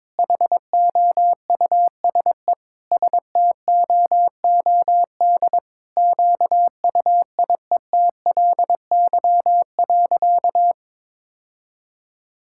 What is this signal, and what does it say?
Morse code, heard as a keyed tone, about 22 wpm, 700 Hz. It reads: HOUSE STOOD QUIETLY.
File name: practice / 22wpm / Sentences with 2-Word Phrases - 22wpm